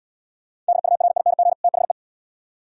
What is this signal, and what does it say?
Morse code, heard as a keyed tone, about 79 wpm, 700 Hz. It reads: DURING THE